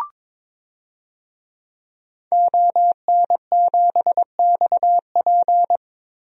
Morse code, heard as a keyed tone, about 22 wpm, 700 Hz. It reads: ON7XP
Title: DXCC-100 Call Signs - 5 Characters - Repeated 2x - 22wpm